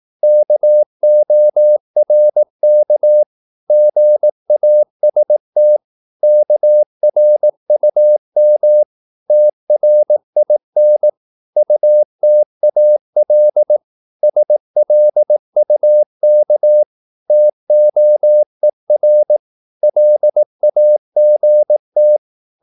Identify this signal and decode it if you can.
Morse code, heard as a keyed tone, about 18 wpm, 600 Hz. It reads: KORK GAST KRUM TRIN UTAL SLUK TOER LAGT
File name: Korte ord DK 18wpm